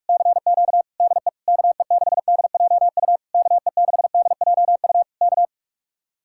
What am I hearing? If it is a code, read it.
Morse code, heard as a keyed tone, about 49 wpm, 700 Hz. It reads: CQ DE KE6DJU KE6DJU K